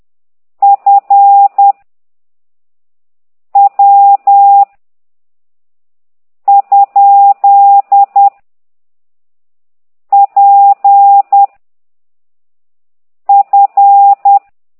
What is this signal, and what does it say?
Morse code, heard as a keyed tone, about 10 wpm, 800 Hz. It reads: F W ? P F
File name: ct2idw / exerc9